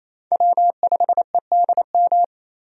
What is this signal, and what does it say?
Morse code, heard as a keyed tone, about 28 wpm, 700 Hz. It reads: W5EDM
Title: USA Call Signs - 5 Characters - 28wpm